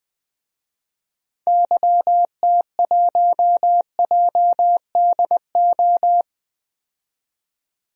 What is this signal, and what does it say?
Morse code, heard as a keyed tone, about 20 wpm, 700 Hz. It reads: YT1JDO
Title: DXCC-100 Call Signs - 6 Characters - 20wpm